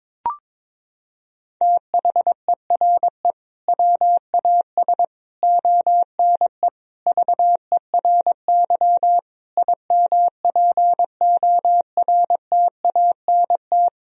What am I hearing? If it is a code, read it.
Morse code, heard as a keyed tone, about 22 wpm, 700 Hz. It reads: THERE WAS ONE VERY IMPORTANT
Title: Sentences from Top 1000 Words - Review Part 3 - 22wpm